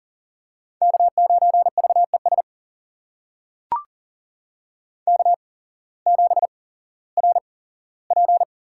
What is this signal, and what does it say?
Morse code, heard as a keyed tone, about 40 wpm, spaced as for 19 wpm, 700 Hz. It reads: K9VES K7RP
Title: USA Call Signs - 5 Characters - 8x Character Spacing - 40wpm